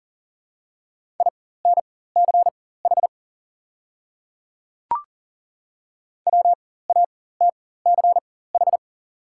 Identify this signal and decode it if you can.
Morse code, heard as a keyed tone, about 39 wpm, spaced as for 18 wpm, 700 Hz. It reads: INCH WATCH
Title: 4x Character Spacing - Top 500 Words - 40wpm